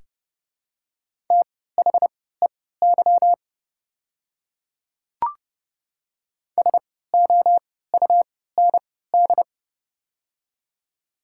Morse code, heard as a keyed tone, about 32 wpm, spaced as for 17 wpm, 700 Hz. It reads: THEY SOUND
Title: 3x Character Spacing - Top 100 Words - 30wpm